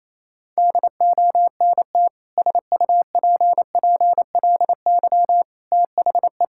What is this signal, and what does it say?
Morse code, heard as a keyed tone, about 28 wpm, 700 Hz. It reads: DONT SUPPLY THE
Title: Sentences from Top 1000 Words - Review Part 5 - 28wpm